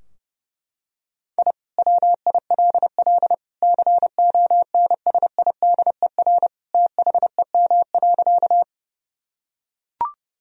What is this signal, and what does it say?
Morse code, heard as a keyed tone, about 32 wpm, 700 Hz. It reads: I WILL CONSIDER THEM.
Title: Sentences from Top 1000 Words - Review Part 4 - 30wpm